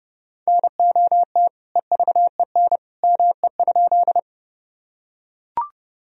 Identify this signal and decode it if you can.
Morse code, heard as a keyed tone, about 30 wpm, 700 Hz. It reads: NOT EVEN ME?
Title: Sentences from Top 1000 Words - Review Part 1 - 30wpm